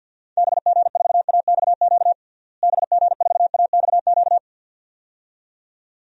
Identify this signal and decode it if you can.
Morse code, heard as a keyed tone, about 52 wpm, 700 Hz. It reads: BG4AXQ BG4AXQ
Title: DXCC-100 Call Signs - 6 Characters - Repeated 2x - 50wpm